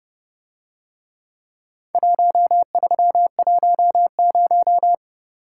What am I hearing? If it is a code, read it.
Morse code, heard as a keyed tone, about 29 wpm, 700 Hz. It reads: 1310